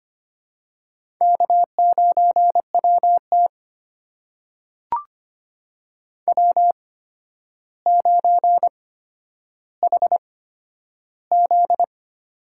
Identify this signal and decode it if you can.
Morse code, heard as a keyed tone, about 25 wpm, spaced as for 11 wpm, 700 Hz. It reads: K9WT W9HZ
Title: USA Call Signs - 4 Characters - 8x Character Spacing - 25wpm